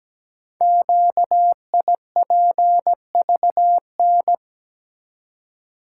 Morse code, heard as a keyed tone, about 17 wpm, 700 Hz. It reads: QIPVN
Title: Code Groups - 5 Letter:Numbers - 17wpm